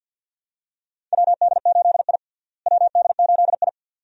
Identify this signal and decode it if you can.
Morse code, heard as a keyed tone, about 52 wpm, 700 Hz. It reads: WD8I WD8I
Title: USA Call Signs - 4 Characters - Repeated 2x - 50wpm